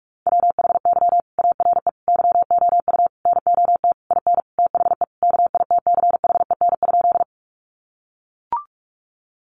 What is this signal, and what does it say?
Morse code, heard as a keyed tone, about 44 wpm, 700 Hz. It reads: WHY ARE YOU NOT IN THE KITCHEN?
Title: Sentences from Top 1400 Words 45wpm